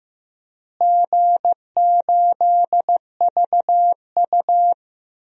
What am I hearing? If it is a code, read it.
Morse code, heard as a keyed tone, about 15 wpm, 700 Hz. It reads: G8VU
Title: DXCC-100 Call Signs - 4 Characters - Repeated 2x - 15wpm